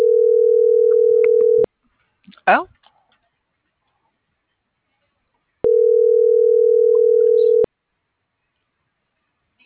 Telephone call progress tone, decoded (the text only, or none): ringback tone